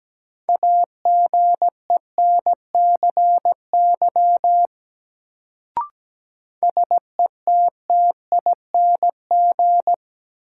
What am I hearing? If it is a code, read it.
Morse code, heard as a keyed tone, about 17 wpm, 700 Hz. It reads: AGENCY SETTING
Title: Sentences from Top 1300 Words - New Word Drill Alt - 17wpm